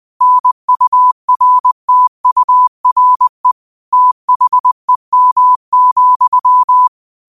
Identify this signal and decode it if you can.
Morse code, heard as a keyed tone, about 20 wpm, 1000 Hz. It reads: NURTURE THEM,
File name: qotdfiles / 20wpm